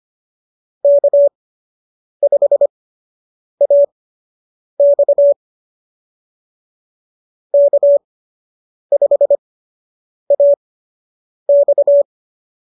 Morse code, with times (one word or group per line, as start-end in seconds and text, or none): 0.8-5.3 K5AX
7.5-12.0 K5AX